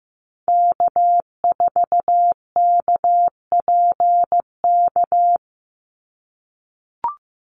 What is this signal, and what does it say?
Morse code, heard as a keyed tone, about 15 wpm, 700 Hz. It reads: K4KPK
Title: NA SOTA Activator Callsigns 15wpm